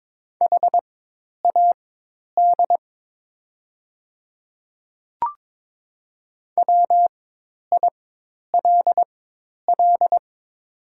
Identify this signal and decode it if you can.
Morse code, heard as a keyed tone, about 22 wpm, spaced as for 10 wpm, 700 Hz. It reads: HAD WILL
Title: 4x Character Spacing - Top 200 Words - 22wpm